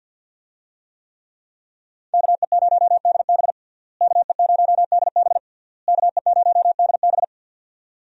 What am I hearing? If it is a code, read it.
Morse code, heard as a keyed tone, about 50 wpm, 700 Hz. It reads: KE0DB KE0DB KE0DB
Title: USA Call Signs - 5 Characters - Repeated 3x - 50wpm